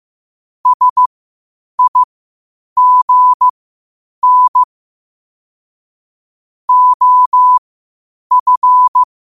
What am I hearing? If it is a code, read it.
Morse code, heard as a keyed tone, about 15 wpm, spaced as for 8 wpm, 1000 Hz. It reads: SIGN OF
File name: qotdfiles / 8wpm